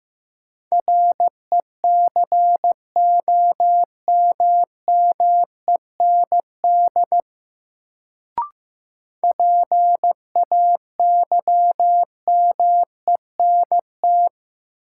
Morse code, heard as a keyed tone, about 15 wpm, 700 Hz. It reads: RECOMMEND PAYMENT
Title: Sentences from Top 1200 Words - New Word Drill - 15wpm